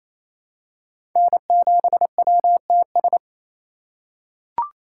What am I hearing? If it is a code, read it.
Morse code, heard as a keyed tone, about 28 wpm, 700 Hz. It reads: N7WTS
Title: USA Call Signs - 2x Character Spacing - 28wpm